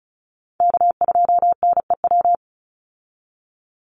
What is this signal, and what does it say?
Morse code, heard as a keyed tone, about 34 wpm, 700 Hz. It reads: K2NEW
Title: USA Call Signs 35wpm